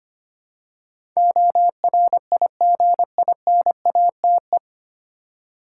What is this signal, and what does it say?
Morse code, heard as a keyed tone, about 25 wpm, 700 Hz. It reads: ORIGINATE
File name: practice / 25wpm / 9 Letter Words - First 3 Letters Given - 25wpm